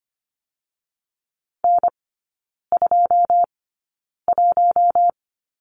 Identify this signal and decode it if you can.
Morse code, heard as a keyed tone, about 25 wpm, spaced as for 9 wpm, 700 Hz. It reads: N21
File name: course / 25:10wpm / 093 Lesson 16e - Letter c - Three Characters 25wpm